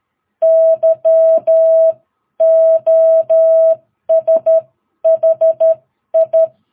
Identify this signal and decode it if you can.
Morse code, heard as a keyed tone, about 13 wpm, spaced as for 11 wpm, 660 Hz. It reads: YOSHI
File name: jh8cbh / 2022cw